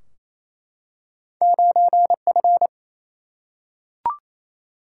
Morse code, heard as a keyed tone, about 28 wpm, 700 Hz. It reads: 9F
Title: Single-Letter Single-Number 28wpm